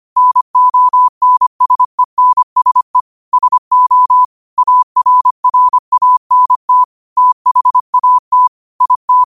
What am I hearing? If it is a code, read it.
Morse code, heard as a keyed tone, about 25 wpm, 1000 Hz. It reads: NONSENSE SO ARRANT THAT IT
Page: New quotes every day in morse code at 25 Words per minute.